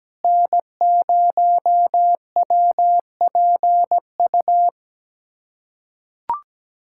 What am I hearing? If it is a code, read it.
Morse code, heard as a keyed tone, about 17 wpm, 700 Hz. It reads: N0WPU
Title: USA Callsign DE Callsign 17wpm